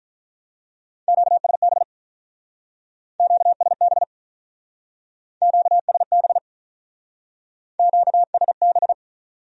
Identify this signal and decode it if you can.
Morse code, heard as a keyed tone, about 45 wpm, 700 Hz. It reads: QSB QSB QSB QSB